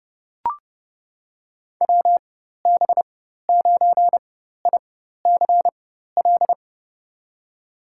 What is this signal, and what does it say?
Morse code, heard as a keyed tone, about 30 wpm, spaced as for 14 wpm, 700 Hz. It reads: WB9ICL
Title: USA Call Signs - 6 Characters - 4x Character Spacing - 30wpm